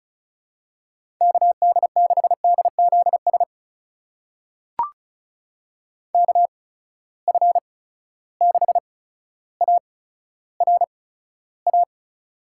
Morse code, heard as a keyed tone, about 36 wpm, spaced as for 15 wpm, 700 Hz. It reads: KD6DZS KF6ARA
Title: USA Call Signs - 6 Characters - 8x Character Spacing - 35wpm